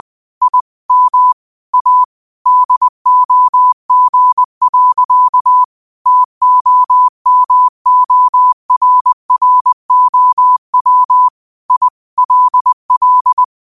Morse code, heard as a keyed tone, about 20 wpm, 1000 Hz. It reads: I M A DOG. TOMORROW I LL